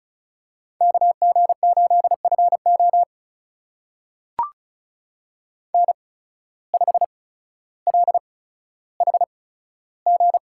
Morse code, heard as a keyed tone, about 36 wpm, spaced as for 15 wpm, 700 Hz. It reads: KG8FO N5LHG
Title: USA Call Signs - 5 Characters - 8x Character Spacing - 35wpm